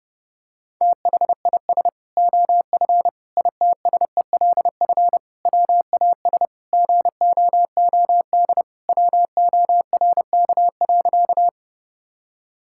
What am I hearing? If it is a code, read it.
Morse code, heard as a keyed tone, about 30 wpm, 700 Hz. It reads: THIS OF ITSELF WAS GOOD WORK.
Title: Sentences from Top 1000 Words - Review Part 5 - 30wpm